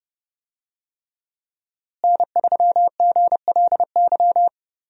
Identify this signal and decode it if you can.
Morse code, heard as a keyed tone, about 30 wpm, 700 Hz. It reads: N3GLY
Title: USA Call Signs - 5 Characters - Repeated 3x - 30wpm